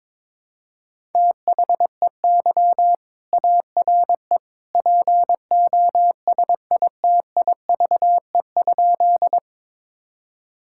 Morse code, heard as a keyed tone, about 22 wpm, 700 Hz. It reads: THEY ARE POSITIVE?
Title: Sentences from Top 1000 Words - Review Part 5 - 22wpm